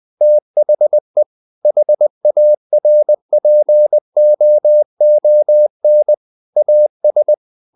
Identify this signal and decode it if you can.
Morse code, heard as a keyed tone, about 20 wpm, 600 Hz. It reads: THE HARPOON AS
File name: the old man and the sea part 2 20wpm